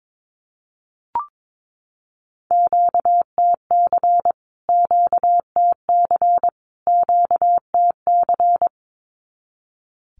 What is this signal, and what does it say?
Morse code, heard as a keyed tone, about 22 wpm, 700 Hz. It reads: QTC QTC QTC
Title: Q-Codes Repeated 3x 22wpm